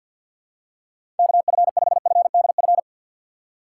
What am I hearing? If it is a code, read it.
Morse code, heard as a keyed tone, about 52 wpm, 700 Hz. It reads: KF5FDF
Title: USA Call Signs - 2x Character Spacing - 50wpm